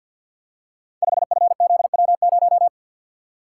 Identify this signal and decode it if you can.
Morse code, heard as a keyed tone, about 53 wpm, 700 Hz. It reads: 5FZW0
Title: Code Groups - 5 Letter:Numbers - 50wpm